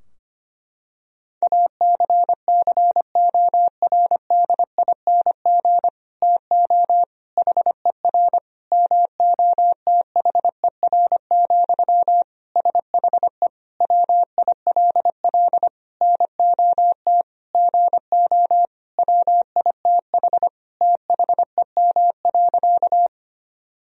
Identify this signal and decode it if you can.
Morse code, heard as a keyed tone, about 25 wpm, 700 Hz. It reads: ACCORDING TO HER MOTHER, SHE WILL NOT GO WITH THEM.
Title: Sentences from Top 1000 Words - Review Part 4 - 25wpm